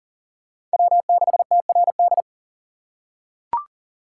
Morse code, heard as a keyed tone, about 37 wpm, 700 Hz. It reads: W6TRD
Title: USA Call Signs - 3x Character Spacing - 40wpm